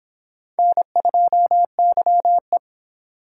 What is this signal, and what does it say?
Morse code, heard as a keyed tone, about 25 wpm, 700 Hz. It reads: N2YE